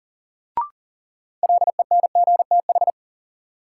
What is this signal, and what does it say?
Morse code, heard as a keyed tone, about 40 wpm, 700 Hz. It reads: LENGTH
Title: Sentences from Top 1000 Words - New Word Drill - 40wpm